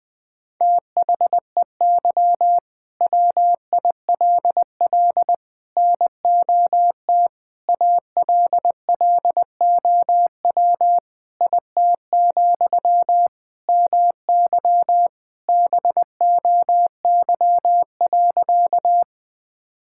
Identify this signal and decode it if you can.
Morse code, heard as a keyed tone, about 20 wpm, 700 Hz. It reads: THEY WILL NOT ALLOW IT, MY BOY.
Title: Sentences from Top 1000 Words - Review Part 5 - 20wpm